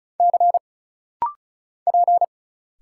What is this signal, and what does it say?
Morse code, heard as a keyed tone, about 36 wpm, 700 Hz. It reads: C P
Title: Single Letters - Rapid-Fire 35wpm